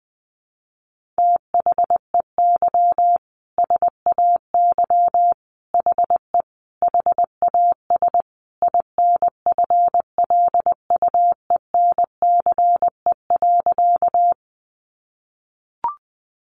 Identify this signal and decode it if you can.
Morse code, heard as a keyed tone, about 20 wpm, 700 Hz. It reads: THEY SAY HE HAS INFLUENCE.
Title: Sentences from Top 1100 Words 20wpm